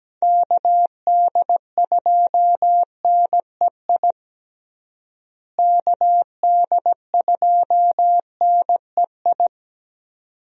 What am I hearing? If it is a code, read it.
Morse code, heard as a keyed tone, about 17 wpm, 700 Hz. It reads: KD2NEI KD2NEI
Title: USA Call Signs - 6 Characters - Repeated 2x - 17wpm